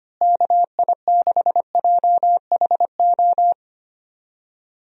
Morse code, heard as a keyed tone, about 25 wpm, 700 Hz. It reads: KI6JHO
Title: USA Call Signs - 6 Characters - Repeated 2x - 25wpm